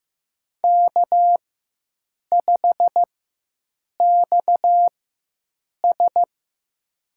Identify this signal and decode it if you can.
Morse code, heard as a keyed tone, about 15 wpm, spaced as for 7 wpm, 700 Hz. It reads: K5XS
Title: USA Call Signs - 4 Characters - 4x Character Spacing - 15wpm